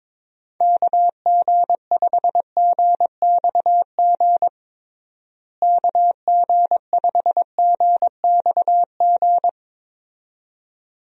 Morse code, heard as a keyed tone, about 21 wpm, 700 Hz. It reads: KG5GXG KG5GXG
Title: USA Call Signs - 6 Characters - Repeated 2x - 22wpm